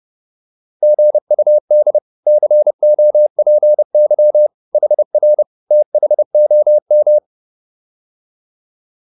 Morse code, heard as a keyed tone, about 30 wpm, 600 Hz. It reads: GUD COPY HR THOM